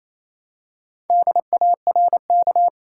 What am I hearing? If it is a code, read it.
Morse code, heard as a keyed tone, about 28 wpm, 700 Hz. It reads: DARK
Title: Four Letter Words - 28wpm